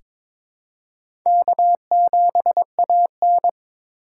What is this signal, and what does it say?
Morse code, heard as a keyed tone, about 23 wpm, 700 Hz. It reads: K7AN
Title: USA Call Signs - 4 Characters - 3x Character Spacing - 22wpm